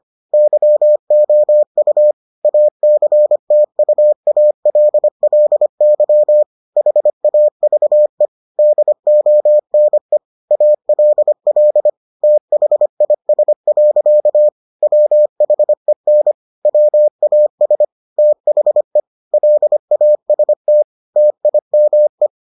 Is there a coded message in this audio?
Yes: Morse code from 0.3 to 22.3 s, heard as a keyed tone, about 25 wpm, 600 Hz. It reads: YOU ACTUALLY HAVE DONE ALL THIS. WHEN WAS THE LAST TIME